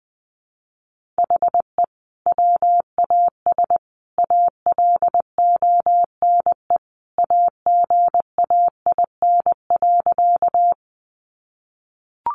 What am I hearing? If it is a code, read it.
Morse code, heard as a keyed tone, about 20 wpm, 700 Hz. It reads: HE WAS ALONE AGAIN.
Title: Sentences from Top 1000 Words 20wpm